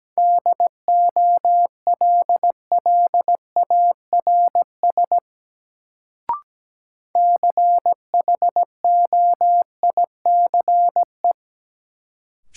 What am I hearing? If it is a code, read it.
Morse code, heard as a keyed tone, about 17 wpm, 700 Hz. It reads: DOLLARS CHOICE
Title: Sets of 5 Words - New Word Drill - 17wpm